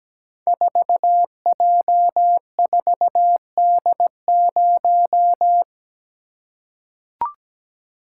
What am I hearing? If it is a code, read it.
Morse code, heard as a keyed tone, about 17 wpm, 700 Hz. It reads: 4J4D0
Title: Code Groups - 5 Letter:Numbers - 17wpm